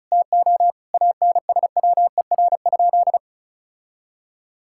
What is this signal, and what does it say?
Morse code, heard as a keyed tone, about 36 wpm, 700 Hz. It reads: TO ANSWER?
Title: Sentences from Top 1000 Words - Review Part 6 - 35wpm